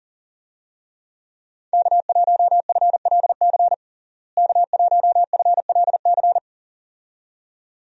Morse code, heard as a keyed tone, about 40 wpm, 700 Hz. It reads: K1FLC K1FLC
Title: USA Call Signs - 5 Characters - Repeated 2x - 40wpm